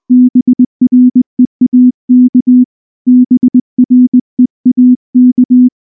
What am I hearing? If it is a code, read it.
Morse code, heard as a keyed tone, about 19 wpm, 250 Hz. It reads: BREAK BREAK
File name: morse / morse-training-20WPM